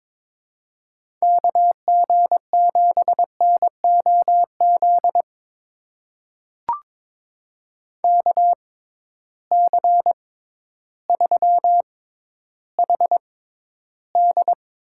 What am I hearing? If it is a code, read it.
Morse code, heard as a keyed tone, about 22 wpm, spaced as for 12 wpm, 700 Hz. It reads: KG7NOZ KC3HD
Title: USA Call Signs - 6 Characters - 6x Character Spacing - 22wpm